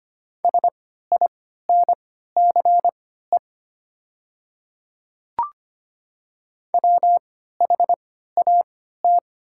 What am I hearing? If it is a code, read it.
Morse code, heard as a keyed tone, about 25 wpm, spaced as for 14 wpm, 700 Hz. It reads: SINCE WHAT
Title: 3x Character Spacing - Top 300 Words - 25wpm